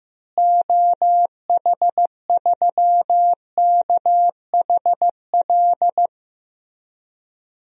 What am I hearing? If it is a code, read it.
Morse code, heard as a keyed tone, about 16 wpm, 700 Hz. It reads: OH3KHL
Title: DXCC-100 Call Signs - 6 Characters - Repeated 3x - 15wpm